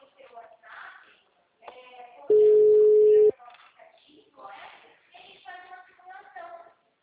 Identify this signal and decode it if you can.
Ringback tone (Europe, 425 Hz), with speech.